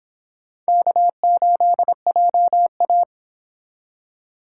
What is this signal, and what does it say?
Morse code, heard as a keyed tone, about 30 wpm, 700 Hz. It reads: K8JA